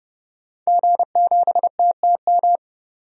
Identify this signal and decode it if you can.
Morse code, heard as a keyed tone, about 31 wpm, 700 Hz. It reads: G7TTM
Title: DXCC-100 Call Signs - 5 Characters - 30wpm